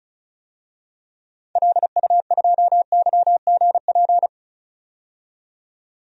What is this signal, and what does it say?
Morse code, heard as a keyed tone, about 36 wpm, 690 Hz. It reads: LU2YGP
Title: DXCC-100 Call Signs - 6 Characters - 35wpm